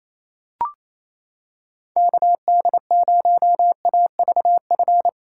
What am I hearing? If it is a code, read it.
Morse code, heard as a keyed tone, about 28 wpm, 700 Hz. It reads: KD0AVF
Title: USA Callsign DE Callsign 28wpm